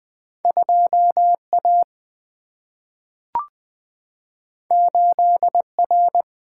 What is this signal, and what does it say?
Morse code, heard as a keyed tone, about 20 wpm, 700 Hz. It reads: 2 A 8 R
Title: Single-Letter Single-Number 20wpm